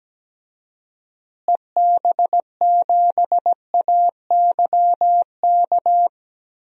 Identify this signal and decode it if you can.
Morse code, heard as a keyed tone, about 17 wpm, 700 Hz. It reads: EB7AYK
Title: DXCC-100 Call Signs - 6 Characters - Repeated 3x - 17wpm